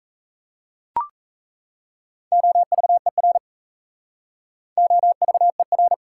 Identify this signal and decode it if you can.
Morse code, heard as a keyed tone, about 38 wpm, 690 Hz. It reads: OVER OVER